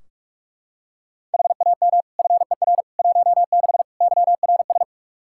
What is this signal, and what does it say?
Morse code, heard as a keyed tone, about 43 wpm, 700 Hz. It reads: HAM FER 16 YRS
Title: K1USN SST QSO Practice 45wpm